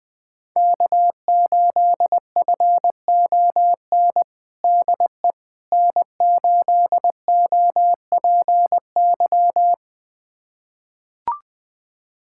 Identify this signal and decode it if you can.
Morse code, heard as a keyed tone, about 20 wpm, 700 Hz. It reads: K8FON DE N8OPY
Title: USA Callsign DE Callsign - 5 Characters - 20wpm